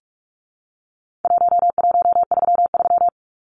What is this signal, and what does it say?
Morse code, heard as a keyed tone, about 44 wpm, 700 Hz. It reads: 1133
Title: Numbers - Contest Style - 4-Digit 45wpm